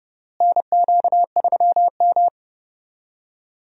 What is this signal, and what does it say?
Morse code, heard as a keyed tone, about 30 wpm, 700 Hz. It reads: NQ3M
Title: USA Call Signs - 4 Characters - Repeated 2x - 30wpm